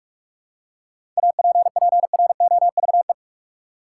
Morse code, heard as a keyed tone, about 46 wpm, 700 Hz. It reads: APPROVE
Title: Seven Letter Words - 45wpm